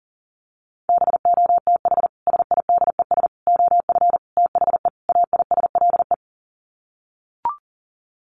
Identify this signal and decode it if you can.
Morse code, heard as a keyed tone, about 40 wpm, 700 Hz. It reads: BOTH SIDES OF THE AISLE
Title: Sets of 5 Words Encore2 40wpm